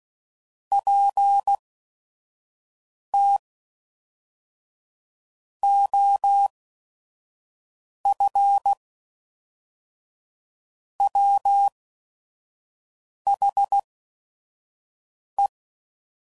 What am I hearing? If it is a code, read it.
Morse code, heard as a keyed tone, about 16 wpm, spaced as for 5 wpm, 800 Hz. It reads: PT OF WHE
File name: CW Code Groups @5WPM / Technical Subjects / AND Gate